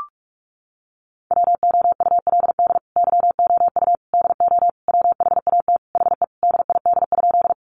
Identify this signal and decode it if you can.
Morse code, heard as a keyed tone, about 44 wpm, 700 Hz. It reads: WOULD YOU DO WHAT HE DID?
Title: Sentences from Top 100 Words 45wpm